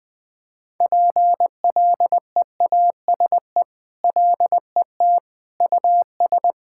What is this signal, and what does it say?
Morse code, heard as a keyed tone, about 21 wpm, 700 Hz. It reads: PLEASE LET US
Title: Sentences with 6-Word Phrases - 20wpm